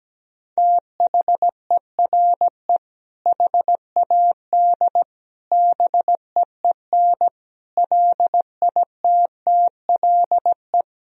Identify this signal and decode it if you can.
Morse code, heard as a keyed tone, about 17 wpm, 700 Hz. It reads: THERE HAD BEEN LITTLE
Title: Sentences from Top 1000 Words - Review Part 1 - 17wpm